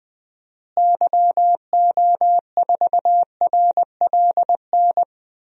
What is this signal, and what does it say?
Morse code, heard as a keyed tone, about 20 wpm, 700 Hz. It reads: YO4RLN
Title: DXCC-100 Call Signs - 6 Characters - Repeated 2x - 20wpm